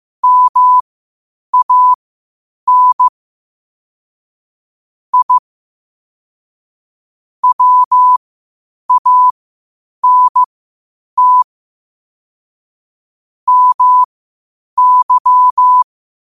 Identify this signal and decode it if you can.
Morse code, heard as a keyed tone, about 15 wpm, spaced as for 8 wpm, 1000 Hz. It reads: MAN I WANT MY
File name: qotdfiles / 8wpm